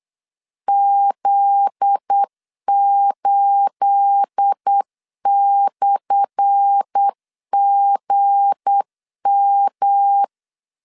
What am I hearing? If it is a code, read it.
Morse code, heard as a keyed tone, about 8 wpm, 800 Hz. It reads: Z8/GM
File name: LEZIONI / 10 SIGNS / 10A SLASH 8wpm 3452